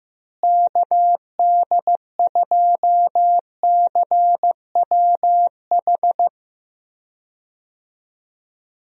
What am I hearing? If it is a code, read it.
Morse code, heard as a keyed tone, about 15 wpm, 700 Hz. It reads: KD2CWH
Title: USA Call Signs - 6 Characters - Repeated 3x - 15wpm